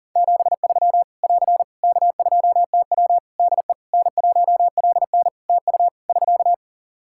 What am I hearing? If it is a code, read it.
Morse code, heard as a keyed tone, about 40 wpm, 700 Hz. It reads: 73 <AR> K2TW DE N1LN TU <SK>
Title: Practice QSO - Exchange 4 - Ending - 40wpm